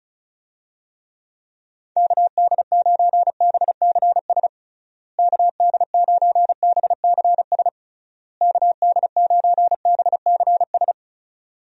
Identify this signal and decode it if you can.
Morse code, heard as a keyed tone, about 34 wpm, 700 Hz. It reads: KD9BCS KD9BCS KD9BCS
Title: USA Call Signs - 6 Characters - Repeated 3x - 35wpm